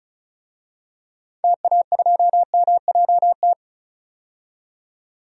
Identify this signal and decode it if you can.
Morse code, heard as a keyed tone, about 36 wpm, 700 Hz. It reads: TA2MJT
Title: DXCC-100 Call Signs - 6 Characters - 35wpm